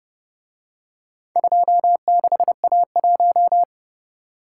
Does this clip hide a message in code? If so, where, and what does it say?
Yes: Morse code from 1.4 to 3.6 s, heard as a keyed tone, about 29 wpm, 700 Hz. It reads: 26A1